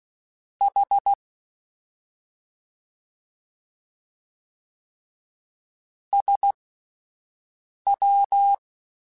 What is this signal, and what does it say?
Morse code, heard as a keyed tone, about 16 wpm, spaced as for 5 wpm, 800 Hz. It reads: H SW
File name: Misc / Misc-mp3s / Jumps @5